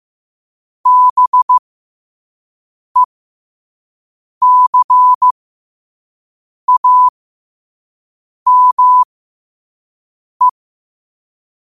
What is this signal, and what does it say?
Morse code, heard as a keyed tone, about 15 wpm, spaced as for 5 wpm, 1000 Hz. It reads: BECAME
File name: qotdfiles / 5wpm